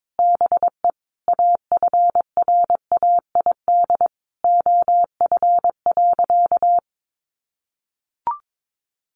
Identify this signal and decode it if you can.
Morse code, heard as a keyed tone, about 22 wpm, 700 Hz. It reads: BE AFRAID OF.
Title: Sentences from Top 1500 Words 22wpm